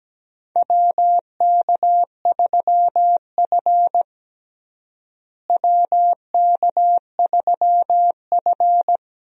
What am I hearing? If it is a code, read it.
Morse code, heard as a keyed tone, about 17 wpm, 700 Hz. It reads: WK3F WK3F
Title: USA Call Signs - 4 Characters - Repeated 2x - 17wpm